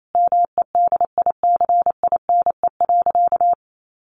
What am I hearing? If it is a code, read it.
Morse code, heard as a keyed tone, about 28 wpm, 690 Hz. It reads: MEDICINE.
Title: Sentences from Top 900 Words 28wpm